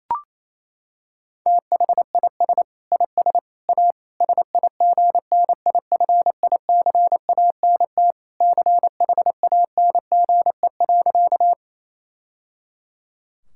Morse code, heard as a keyed tone, about 27 wpm, 700 Hz. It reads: THIS IS A SIGNIFICANT CHANGE.
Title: Sentences from Top 1000 Words - Review Part 6 - 28wpm